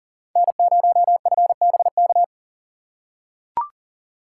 Morse code, heard as a keyed tone, about 40 wpm, 700 Hz. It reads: N0FBK
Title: USA Call Signs - 5 Characters - 6x Character Spacing - 40wpm